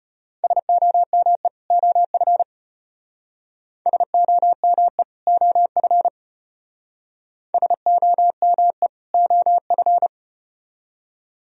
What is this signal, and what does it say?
Morse code, heard as a keyed tone, about 33 wpm, 700 Hz. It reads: SOME OF SOME OF SOME OF